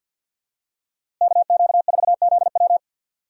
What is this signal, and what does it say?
Morse code, heard as a keyed tone, about 50 wpm, 700 Hz. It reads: KQ4ZW
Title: USA Call Signs - 5 Characters - Repeated 2x - 50wpm